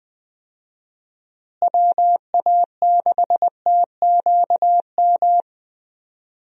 Morse code, heard as a keyed tone, about 21 wpm, 700 Hz. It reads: WA6TQM
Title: USA Call Signs - 6 Characters - Repeated 2x - 20wpm